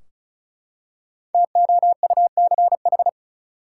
Morse code, heard as a keyed tone, about 34 wpm, 700 Hz. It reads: TOUCH
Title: Five Letter Words - 35wpm